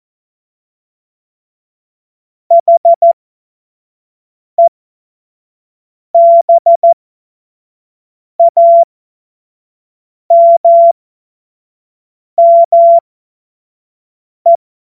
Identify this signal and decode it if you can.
Morse code, heard as a keyed tone, about 14 wpm, spaced as for 5 wpm, 680 Hz. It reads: HEBAMME